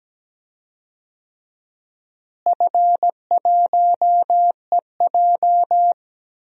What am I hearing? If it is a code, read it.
Morse code, heard as a keyed tone, about 17 wpm, 700 Hz. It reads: F1EJ
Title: DXCC-100 Call Signs - 4 Characters - Repeated 2x - 17wpm